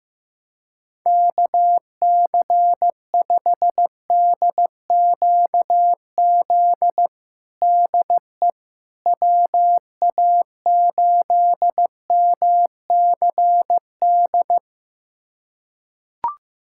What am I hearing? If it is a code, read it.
Morse code, heard as a keyed tone, about 15 wpm, 700 Hz. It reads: KC5DQZ DE WA8MCD
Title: USA Callsign DE Callsign 15wpm